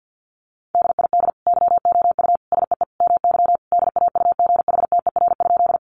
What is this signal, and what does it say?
Morse code, heard as a keyed tone, about 50 wpm, 700 Hz. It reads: DID YOU SEE MY DAUGHTER?
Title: Sentences from Top 1500 Words 50wpm